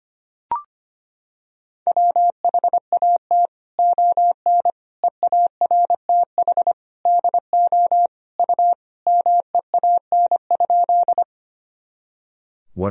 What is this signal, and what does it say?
Morse code, heard as a keyed tone, about 25 wpm, 700 Hz. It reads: WHAT ON EARTH DO U MEAN?
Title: Sentences from Top 200 Words and Abbreviations 25wpm